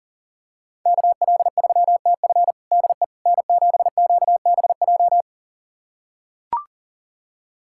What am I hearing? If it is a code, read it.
Morse code, heard as a keyed tone, about 40 wpm, 690 Hz. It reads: KL3TF DE N7QBJ
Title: USA Callsign DE Callsign - 5 Characters - 40wpm